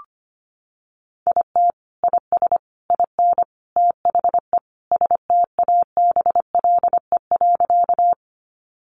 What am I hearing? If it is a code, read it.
Morse code, heard as a keyed tone, about 25 wpm, 700 Hz. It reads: IT IS IN THE STABLE.
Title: Sentences from Top 1500 Words 25wpm